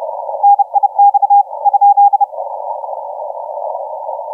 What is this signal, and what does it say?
Morse code, heard as a keyed tone, about 31 wpm, 810 Hz. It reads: NIX ?
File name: VU7AG 0636UTC 10m